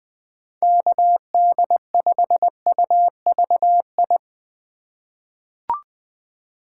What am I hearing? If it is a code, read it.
Morse code, heard as a keyed tone, about 20 wpm, 700 Hz. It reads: KD5UVI
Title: USA Call Signs - 6 Characters - 3x Character Spacing - 20wpm